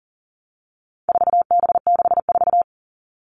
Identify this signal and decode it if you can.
Morse code, heard as a keyed tone, about 40 wpm, 700 Hz. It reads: 4B64
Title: All Cut Numbers - 4-Digit 40wpm